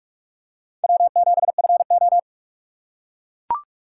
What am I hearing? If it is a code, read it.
Morse code, heard as a keyed tone, about 42 wpm, 700 Hz. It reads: W7FO